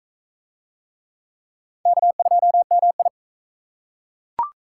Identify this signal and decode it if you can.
Morse code, heard as a keyed tone, about 40 wpm, 700 Hz. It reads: K2MI